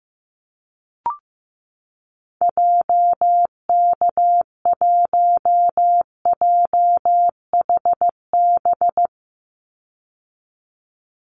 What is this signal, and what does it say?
Morse code, heard as a keyed tone, about 15 wpm, 700 Hz. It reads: JK1JHB
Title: Call Signs Repeated 2x - 15wpm